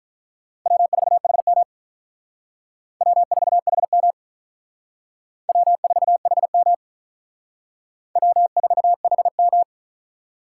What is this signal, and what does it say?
Morse code, heard as a keyed tone, about 43 wpm, 700 Hz. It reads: W4HM W4HM W4HM W4HM
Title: Speed Racing - USA Call Signs 35wpm